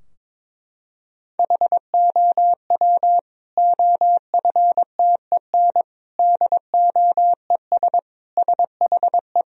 Morse code, heard as a keyed tone, about 22 wpm, 700 Hz. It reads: HOW OFTEN DOES SHE
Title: Sentences from Top 1000 Words - Review Part 3 - 22wpm